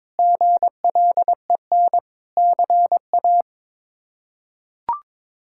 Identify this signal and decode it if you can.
Morse code, heard as a keyed tone, about 22 wpm, 700 Hz. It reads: GLEN CA
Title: K1USN SST QSO Practice - 22wpm